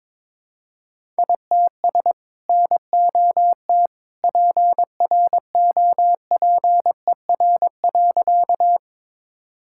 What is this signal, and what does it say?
Morse code, heard as a keyed tone, about 22 wpm, 700 Hz. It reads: ITS NOT PROPER.
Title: Sentences from Top 1000 Words - Review Part 5 - 22wpm